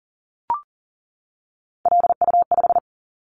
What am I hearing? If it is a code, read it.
Morse code, heard as a keyed tone, about 38 wpm, 700 Hz. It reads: LU5
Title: Callsign Prefixes for Top 50 DXCC - 40wpm